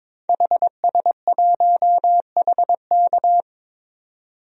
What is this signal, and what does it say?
Morse code, heard as a keyed tone, about 22 wpm, 700 Hz. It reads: HS1HK
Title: DXCC-100 Call Signs - 5 Characters - 22wpm